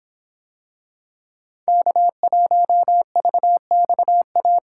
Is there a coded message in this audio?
Yes: Morse code from 1.7 to 4.6 s, heard as a keyed tone, about 24 wpm, 700 Hz. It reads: K1VXA